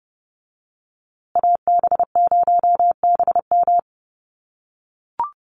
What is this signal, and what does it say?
Morse code, heard as a keyed tone, about 29 wpm, 700 Hz. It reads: AB0BM
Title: USA Call Signs Repeated 2x 30wpm